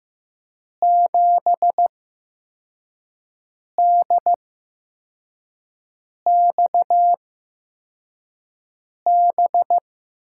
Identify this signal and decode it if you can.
Morse code, heard as a keyed tone, about 15 wpm, spaced as for 4 wpm, 700 Hz. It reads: 7DXB
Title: USA Call Signs - 6 Characters - 8x Character Spacing - 15wpm